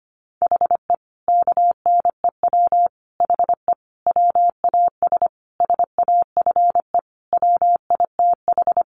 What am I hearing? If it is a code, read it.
Morse code, heard as a keyed tone, about 25 wpm, 700 Hz. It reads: HE KNEW HE WAS SAFE WITH
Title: Sentences from Top 900 Words 25wpm